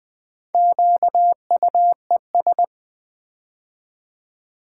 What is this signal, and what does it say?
Morse code, heard as a keyed tone, about 20 wpm, 700 Hz. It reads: QUES
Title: Build A Word - 4 Letters At A Time - 8-Letter Words - 20wpm